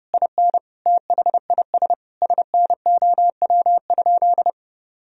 Morse code, heard as a keyed tone, about 30 wpm, 700 Hz. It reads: IN THIS SNOW?
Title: Sentences from Top 1000 Words - Review Part 3 - 30wpm